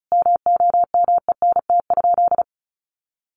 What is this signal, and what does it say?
Morse code, heard as a keyed tone, about 34 wpm, 700 Hz. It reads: MOMENT?
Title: Sentences from Top 1000 Words 35wpm